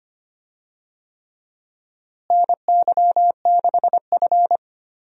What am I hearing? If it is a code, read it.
Morse code, heard as a keyed tone, about 25 wpm, 700 Hz. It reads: NY6F
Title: USA Call Signs - 4 Characters - Repeated 3x - 25wpm